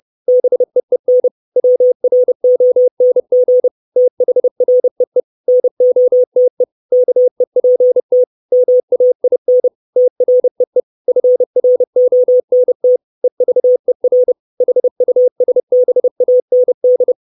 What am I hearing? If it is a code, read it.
Morse code, heard as a keyed tone, about 30 wpm, 500 Hz. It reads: BEEN WRONG THREE NOTE KEPT MAIN TREE FRONT EVER HUSBAND
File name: 632 Words 30WPM